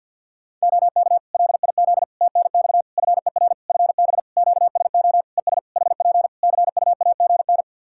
Morse code, heard as a keyed tone, about 49 wpm, 700 Hz. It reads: OK LIZ TNX FER FB QSO ES HP CUAGN
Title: Practice QSO - Exchange 4 - Ending - 50wpm